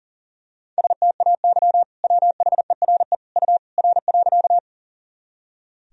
Morse code, heard as a keyed tone, about 40 wpm, 700 Hz. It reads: STAY WHERE U R.
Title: Sentences from Top 500 Words and Abbreviations 40wpm